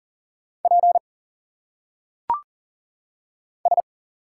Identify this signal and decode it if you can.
Morse code, heard as a keyed tone, about 38 wpm, 700 Hz. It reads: P S
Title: Single Letters - Phonetic Pronunciation 40wpm